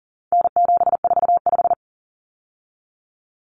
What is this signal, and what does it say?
Morse code, heard as a keyed tone, about 40 wpm, 700 Hz. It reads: N745
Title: Common Cut Numbers - 4-Digit 40wpm